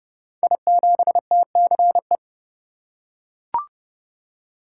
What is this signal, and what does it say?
Morse code, heard as a keyed tone, about 30 wpm, 700 Hz. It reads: I7TCE
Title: Code Groups - 5 Letter:Numbers - 30wpm